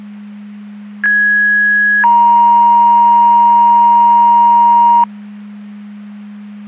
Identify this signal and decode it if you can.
Two-tone page, heard as a one-second tone, then a three-second tone, about 1650 then 940 Hz. It is heard from 1.0 to 5.0 s.
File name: Portage 154.385 Paging